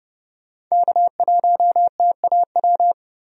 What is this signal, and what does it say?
Morse code, heard as a keyed tone, about 32 wpm, 700 Hz. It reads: K1TAW
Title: USA Call Signs - 5 Characters - 30wpm